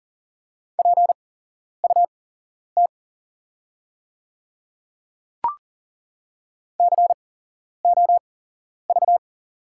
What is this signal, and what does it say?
Morse code, heard as a keyed tone, about 38 wpm, spaced as for 11 wpm, 700 Hz. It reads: PUT COV
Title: 8x Character Spacing - Top 300 Words - 40wpm